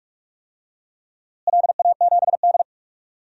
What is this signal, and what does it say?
Morse code, heard as a keyed tone, about 42 wpm, 700 Hz. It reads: LA7D
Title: DXCC-100 Call Signs - 4 Characters - Repeated 2x - 45wpm